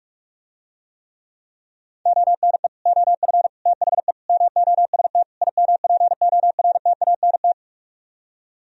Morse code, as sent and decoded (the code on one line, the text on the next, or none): --- -. / --- ..-. / - .... / -- --- ... - / .. -- .--. --- .-. - .- -. -
ONE OF THE MOST IMPORTANT